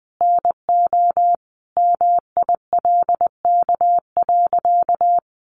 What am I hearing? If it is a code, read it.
Morse code, heard as a keyed tone, about 20 wpm, 700 Hz. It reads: NO MILK.
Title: Sentences from Top 1500 Words 20wpm